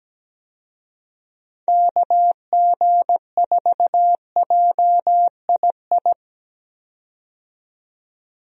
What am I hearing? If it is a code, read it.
Morse code, heard as a keyed tone, about 17 wpm, 700 Hz. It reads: KG4JII
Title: USA Call Signs - 6 Characters - 17wpm